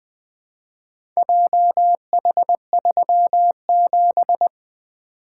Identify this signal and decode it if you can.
Morse code, heard as a keyed tone, about 20 wpm, 700 Hz. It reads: JH37
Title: Four Letter-Number Combinations - 20wpm